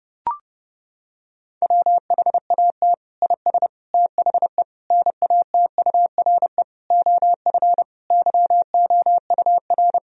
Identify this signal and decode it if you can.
Morse code, heard as a keyed tone, about 30 wpm, 700 Hz. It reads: WHAT IS THE NATURE OF YOUR
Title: Sentences from Top 1000 Words - Review Part 4 - 30wpm